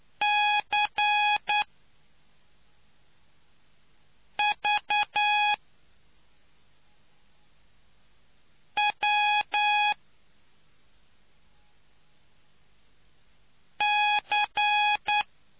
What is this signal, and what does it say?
Morse code, heard as a keyed tone, about 9 wpm, spaced as for 3 wpm, 830 Hz. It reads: CV W C